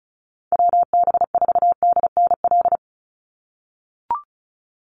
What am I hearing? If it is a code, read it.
Morse code, heard as a keyed tone, about 34 wpm, 700 Hz. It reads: WB4DNL
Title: Call Signs - 35wpm